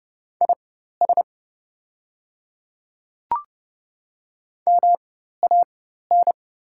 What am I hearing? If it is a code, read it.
Morse code, heard as a keyed tone, about 30 wpm, spaced as for 14 wpm, 700 Hz. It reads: IS MAN
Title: 4x Character Spacing - Top 300 Words - 30wpm